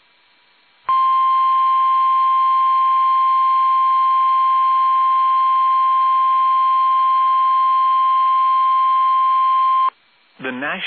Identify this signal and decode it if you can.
Emergency Alert System (SAME) tones (the 1050 Hz attention tone), with speech. It is heard from 0.8 to 9.9 s.